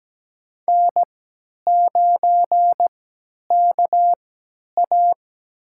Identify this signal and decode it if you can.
Morse code, heard as a keyed tone, about 17 wpm, spaced as for 10 wpm, 700 Hz. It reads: N9KA
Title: USA Call Signs - 3x Character Spacing - 17wpm